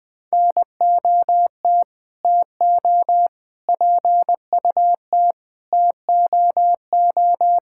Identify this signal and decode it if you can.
Morse code, heard as a keyed tone, about 20 wpm, 700 Hz. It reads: NOT TO PUT TOO
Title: Sets of 8 Words - 20wpm